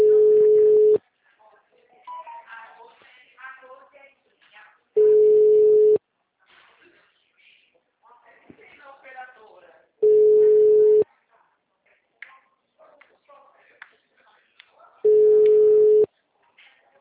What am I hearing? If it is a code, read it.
Ringback tone (Europe, 425 Hz), with speech.